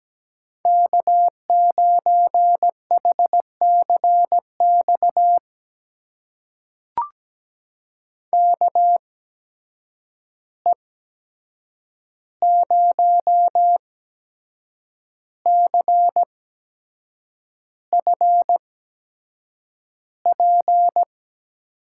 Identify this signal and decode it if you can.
Morse code, heard as a keyed tone, about 17 wpm, spaced as for 7 wpm, 700 Hz. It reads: K9HCX KE0CFP
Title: USA Call Signs - 8x Character Spacing - 17wpm